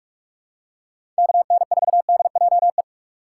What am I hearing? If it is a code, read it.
Morse code, heard as a keyed tone, about 42 wpm, 700 Hz. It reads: KN4DJE
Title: USA Call Signs - 2x Character Spacing - 45wpm